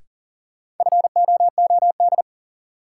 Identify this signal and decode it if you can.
Morse code, heard as a keyed tone, about 37 wpm, 700 Hz. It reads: FOOD
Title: Sentences from Top 300 Words - New Word Drill - 40wpm